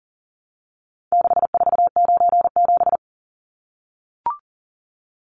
Morse code, heard as a keyed tone, about 40 wpm, 700 Hz. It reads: 6497
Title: Numbers - 4-Digit 40wpm